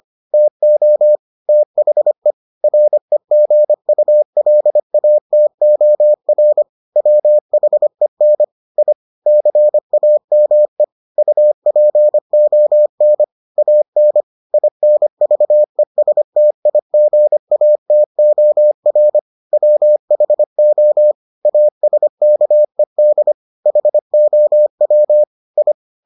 Morse code, heard as a keyed tone, about 25 wpm, 600 Hz. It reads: TO THE REGULATOR WHEN I CAME UPON AN INVESTIGATOR WHO ASKED HOW I